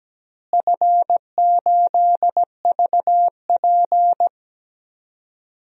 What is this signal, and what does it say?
Morse code, heard as a keyed tone, about 17 wpm, 700 Hz. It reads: F8VP
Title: DXCC-100 Call Signs - 4 Characters - Repeated 3x - 17wpm